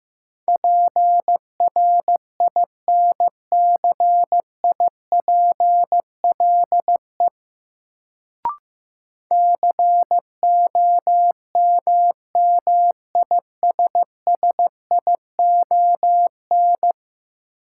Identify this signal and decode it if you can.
Morse code, heard as a keyed tone, about 15 wpm, 700 Hz. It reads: PRINCIPLE COMMISSION
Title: Sentences from Top 1400 Words - New Word Drill - 15wpm